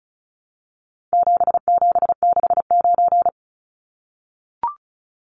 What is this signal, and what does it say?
Morse code, heard as a keyed tone, about 34 wpm, 700 Hz. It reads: 7769
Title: Numbers - 4-Digit 35wpm